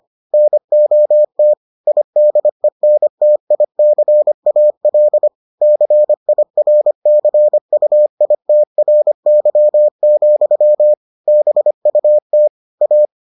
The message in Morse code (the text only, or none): NOT IDENTICAL CIRCUITRY, BUT A